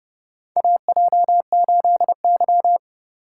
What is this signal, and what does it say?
Morse code, heard as a keyed tone, about 30 wpm, 700 Hz. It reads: AJ8Y
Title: USA Call Signs - 4 Characters - 3x Character Spacing - 30wpm